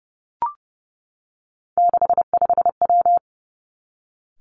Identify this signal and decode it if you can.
Morse code, heard as a keyed tone, about 30 wpm, 700 Hz. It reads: 65W
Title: Three Challenging Letter-Number Combinations 30wpm